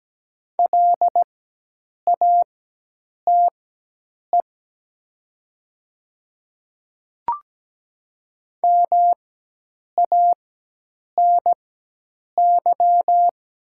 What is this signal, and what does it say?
Morse code, heard as a keyed tone, about 17 wpm, spaced as for 8 wpm, 700 Hz. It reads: LATE MANY
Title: 4x Character Spacing - Top 300 Words - 17wpm